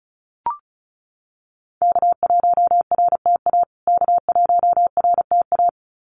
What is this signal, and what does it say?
Morse code, heard as a keyed tone, about 34 wpm, 700 Hz. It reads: K1RTA K1RTA
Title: USA Call Signs Repeated 2x 35wpm